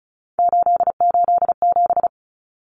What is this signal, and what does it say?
Morse code, heard as a keyed tone, about 34 wpm, 700 Hz. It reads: 887
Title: Common Cut Numbers - 3-Digit 35wpm